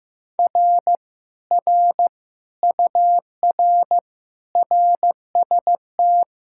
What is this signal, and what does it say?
Morse code, heard as a keyed tone, about 15 wpm, 700 Hz. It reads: R R UR RST
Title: Speed-Racing Common QSO Phrases Encore - 15wpm